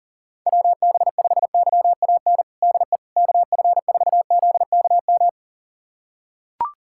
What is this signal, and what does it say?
Morse code, heard as a keyed tone, about 40 wpm, 700 Hz. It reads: WB5YAN DE KF4ZKM
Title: USA Callsign DE Callsign 40wpm